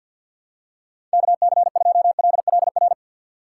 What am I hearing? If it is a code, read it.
Morse code, heard as a keyed tone, about 53 wpm, 700 Hz. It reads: KC2LLR
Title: USA Call Signs - 6x Character Spacing - 50wpm